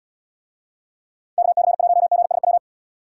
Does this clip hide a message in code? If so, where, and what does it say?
Yes: Morse code from 1.4 to 2.6 s, heard as a keyed tone, about 75 wpm, 700 Hz. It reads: KC1GSW